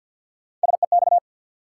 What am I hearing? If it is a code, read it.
Morse code, heard as a keyed tone, about 52 wpm, 700 Hz. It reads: SEX
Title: Sentences from Top 1600 Words - New Word Drill Alt - 50wpm